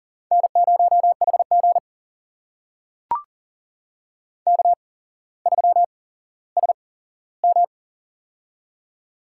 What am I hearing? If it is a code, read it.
Morse code, heard as a keyed tone, about 42 wpm, spaced as for 17 wpm, 690 Hz. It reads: N0HG K3SM
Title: USA Call Signs - 4 Characters - 8x Character Spacing - 40wpm